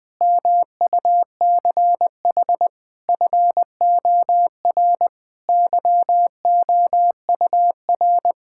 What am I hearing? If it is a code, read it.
Morse code, heard as a keyed tone, about 21 wpm, 700 Hz. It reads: MUCH FOR YOUR
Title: Sentences with 5-Word Phrases - Spoken Beginning - 20wpm